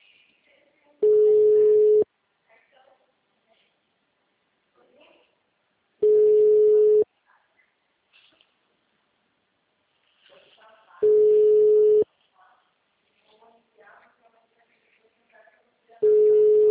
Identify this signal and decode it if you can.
Ringback tone (Europe, 425 Hz), with speech.